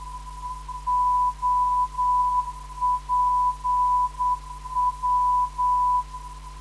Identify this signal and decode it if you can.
Morse code, heard as a keyed tone, about 9 wpm, 990 Hz. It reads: OPW
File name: ndb zvok / OPW - 267.5kHz